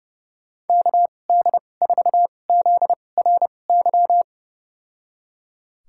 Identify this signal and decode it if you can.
Morse code, heard as a keyed tone, about 31 wpm, spaced as for 22 wpm, 700 Hz. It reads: KD4ZRY
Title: USA Call Signs - 6 Characters - 2x Character Spacing - 30wpm